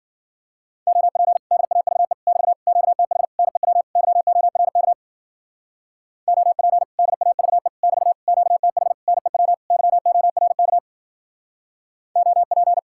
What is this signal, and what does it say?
Morse code, heard as a keyed tone, about 48 wpm, 700 Hz. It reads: OP DAVE <BT> QTH NEW YORK OP DAVE <BT> QTH NEW YORK OP
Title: Speed-Racing Common QSO Phrases Encore - 40wpm